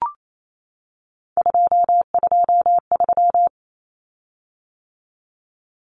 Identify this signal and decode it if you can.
Morse code, heard as a keyed tone, about 28 wpm, 700 Hz. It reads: 223
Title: Common Cut Numbers - 3-Digit 28wpm